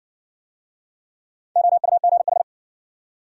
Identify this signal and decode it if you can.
Morse code, heard as a keyed tone, about 53 wpm, 700 Hz. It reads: OUGH